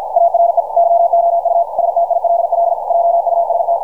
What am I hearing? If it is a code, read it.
Morse code, heard as a keyed tone, about 52 wpm, 680 Hz. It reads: ARE YOU STILL JUST
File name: archived / CRSnet